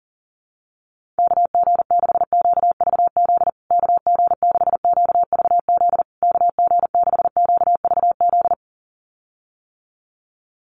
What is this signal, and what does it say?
Morse code, heard as a keyed tone, about 40 wpm, 700 Hz. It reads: KG6QVZ KG6QVZ KG6QVZ
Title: USA Call Signs Repeated 3x 40wpm